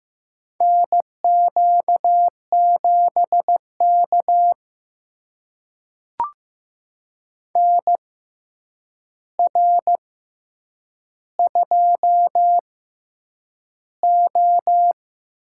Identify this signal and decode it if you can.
Morse code, heard as a keyed tone, about 15 wpm, spaced as for 8 wpm, 700 Hz. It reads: NQ7K NR2O
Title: USA Call Signs - 4 Characters - 6x Character Spacing - 15wpm